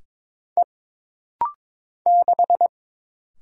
Morse code, heard as a keyed tone, about 22 wpm, 700 Hz. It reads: E 6
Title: Single Letter-Number Alt - Rapid-Fire 22wpm